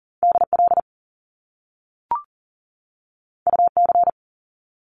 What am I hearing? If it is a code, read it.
Morse code, heard as a keyed tone, about 40 wpm, 700 Hz. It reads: DL UC
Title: Top Two Letter Combinations in Words 40wpm